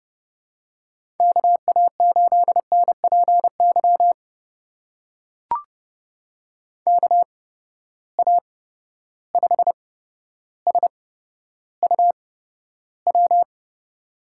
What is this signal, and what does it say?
Morse code, heard as a keyed tone, about 31 wpm, spaced as for 13 wpm, 700 Hz. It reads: KA8NPY KA5SUW
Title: USA Call Signs - 6 Characters - 8x Character Spacing - 30wpm